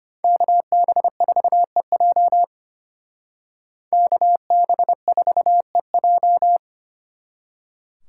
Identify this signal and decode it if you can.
Morse code, heard as a keyed tone, about 32 wpm, 700 Hz. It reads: KB4EJ KB4EJ